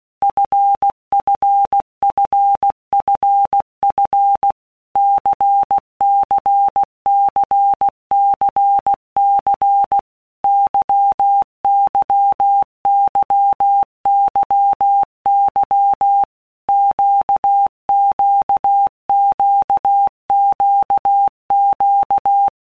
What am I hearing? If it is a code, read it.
Morse code, heard as a keyed tone, about 16 wpm, 800 Hz. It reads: FFFFF CCCCC YYYYY QQQQQ
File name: Basic Letter Groups By Blocks / Groups@15 / LFCYQ@15